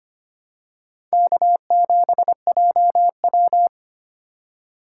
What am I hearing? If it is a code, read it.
Morse code, heard as a keyed tone, about 25 wpm, 700 Hz. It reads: K7JW
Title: USA Call Signs - 4 Characters - 2x Character Spacing - 25wpm